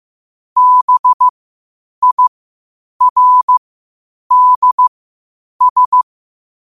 Morse code, heard as a keyed tone, about 15 wpm, spaced as for 8 wpm, 1000 Hz. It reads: BIRDS